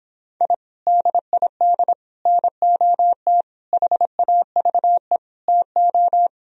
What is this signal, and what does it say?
Morse code, heard as a keyed tone, about 26 wpm, 700 Hz. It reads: I DID NOT HAVE TO